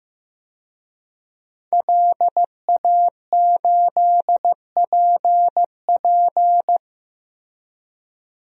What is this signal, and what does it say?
Morse code, heard as a keyed tone, about 15 wpm, 700 Hz. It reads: LA8PP
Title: DXCC-100 Call Signs - 5 Characters - 15wpm